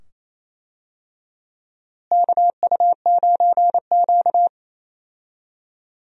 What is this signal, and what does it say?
Morse code, heard as a keyed tone, about 27 wpm, 700 Hz. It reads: KU9Q
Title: USA Call Signs - 4 Characters - Repeated 2x - 28wpm